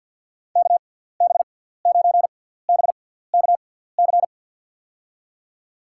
Morse code, heard as a keyed tone, about 49 wpm, spaced as for 17 wpm, 700 Hz. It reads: KB9BKC
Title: USA Call Signs - 6 Characters - 6x Character Spacing - 50wpm